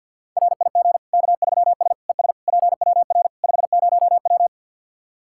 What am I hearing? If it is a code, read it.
Morse code, heard as a keyed tone, about 50 wpm, 700 Hz. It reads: RIG K3S ES PWR 50W
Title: Practice QSO - Exchange 2 - Info - 50wpm